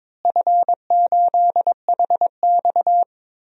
Morse code, heard as a keyed tone, about 22 wpm, 700 Hz. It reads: F8HX
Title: DXCC-100 Call Signs - 4 Characters - Repeated 2x - 22wpm